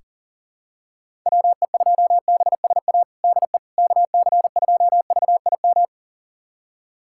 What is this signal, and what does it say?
Morse code, heard as a keyed tone, about 40 wpm, 700 Hz. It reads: WE2BSA DE KC2VIM
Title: USA Callsign DE Callsign - 6 Characters - 40wpm